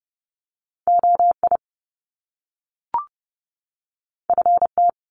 Morse code, heard as a keyed tone, about 30 wpm, 700 Hz. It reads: OI FT
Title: Top Two Letter Combinations in Words 30wpm